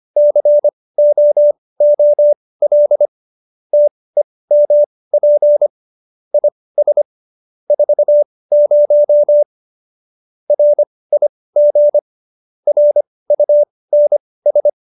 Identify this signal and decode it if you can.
Morse code, heard as a keyed tone, about 25 wpm, spaced as for 18 wpm, 590 Hz. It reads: COOL TEMP IS 40 RIG RUNS
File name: qso201b 18